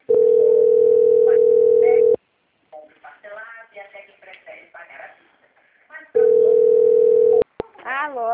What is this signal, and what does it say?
Ringback tone (North America), 440 + 480 Hz, with speech.
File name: out- - - - - .18723